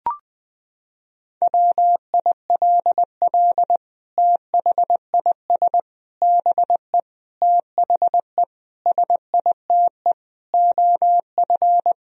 Morse code, heard as a keyed tone, about 20 wpm, 700 Hz. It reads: WILL THIS BE THE SITE OF
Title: Sentences from Top 1000 Words - Review Part 1 - 20wpm